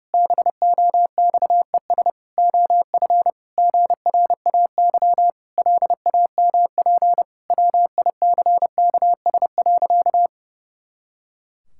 Morse code, heard as a keyed tone, about 32 wpm, 700 Hz. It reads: BOXES OF GRAY LAMP WICKS.
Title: Pangram Challenge v2 - 30wpm